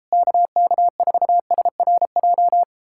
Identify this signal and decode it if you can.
Morse code, heard as a keyed tone, about 37 wpm, 700 Hz. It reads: KK4SRJ